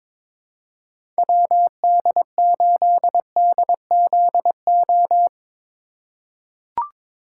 Morse code, heard as a keyed tone, about 22 wpm, 700 Hz. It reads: WD8DZO
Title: USA Call Signs - 4x Character Spacing - 22wpm